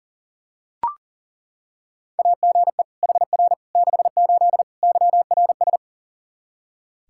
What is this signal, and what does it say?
Morse code, heard as a keyed tone, about 40 wpm, 700 Hz. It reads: AGE HR 68 YRS
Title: Practice QSO - Exchange 3 - More Info - 40wpm